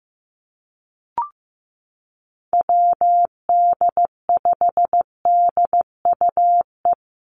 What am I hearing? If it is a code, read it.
Morse code, heard as a keyed tone, about 15 wpm, 700 Hz. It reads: WD5DUE
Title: USA Call Signs Repeated 3x 15wpm